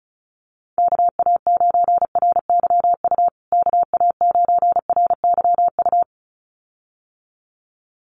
Morse code, heard as a keyed tone, about 34 wpm, 700 Hz. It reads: KA9RYU KA9RYU
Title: USA Call Signs Repeated 2x 35wpm